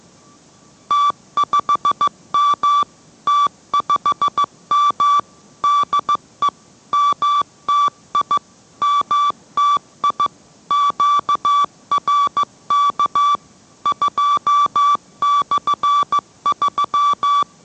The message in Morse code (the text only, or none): T5M T5M DE MTI MTI QRK 2/3